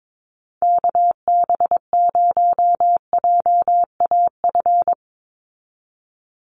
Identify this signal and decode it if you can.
Morse code, heard as a keyed tone, about 22 wpm, 700 Hz. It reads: KB0JAF
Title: USA Call Signs 22wpm